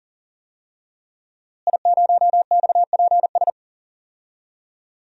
Morse code, heard as a keyed tone, about 40 wpm, 700 Hz. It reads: I0XPS
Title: DXCC-100 Call Signs - 5 Characters - Repeated 2x - 40wpm